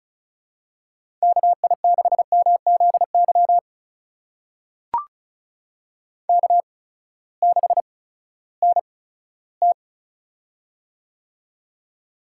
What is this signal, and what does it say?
Morse code, heard as a keyed tone, about 34 wpm, spaced as for 17 wpm, 700 Hz. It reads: KI6MZY K6NT
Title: USA Call Signs - 8x Character Spacing - 35wpm